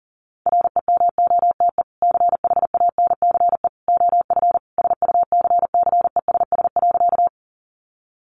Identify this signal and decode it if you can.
Morse code, heard as a keyed tone, about 40 wpm, 700 Hz. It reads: REMOTE CHANCE OF SUCCESS.
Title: Sentences from Top 1300 Words 40wpm